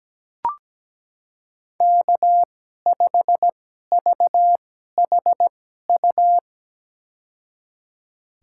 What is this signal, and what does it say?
Morse code, heard as a keyed tone, about 17 wpm, spaced as for 12 wpm, 700 Hz. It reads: K5VHU
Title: USA Call Signs - 5 Characters - 2x Character Spacing - 17wpm